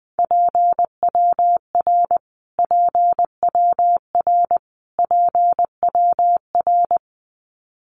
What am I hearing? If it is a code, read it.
Morse code, heard as a keyed tone, about 20 wpm, 700 Hz. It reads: PWR PWR PWR
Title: Abbreviations Repeated 3x - 20wpm